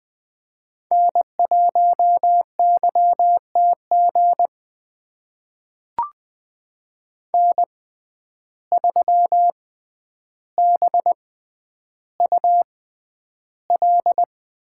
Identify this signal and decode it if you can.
Morse code, heard as a keyed tone, about 20 wpm, spaced as for 10 wpm, 700 Hz. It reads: N1YTG N3BUL
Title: USA Call Signs - 5 Characters - 6x Character Spacing - 20wpm